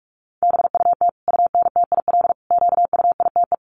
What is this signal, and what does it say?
Morse code, heard as a keyed tone, about 44 wpm, 700 Hz. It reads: BUT UNTIL QUITE
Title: Sentences from Top 1200 Words 45wpm